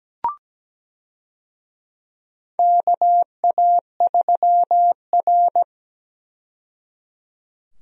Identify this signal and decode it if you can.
Morse code, heard as a keyed tone, about 17 wpm, 700 Hz. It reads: KA3R
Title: USA Call Signs - 4 Characters - 17wpm